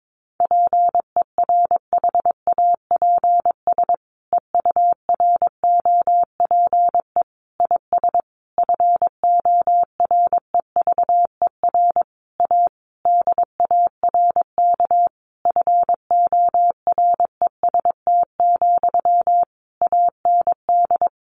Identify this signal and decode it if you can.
Morse code, heard as a keyed tone, about 22 wpm, 700 Hz. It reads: PERHAPS EUROPE IS FOREVER A DARK FOREST, AND
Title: A Sound Of Thunder 22wpm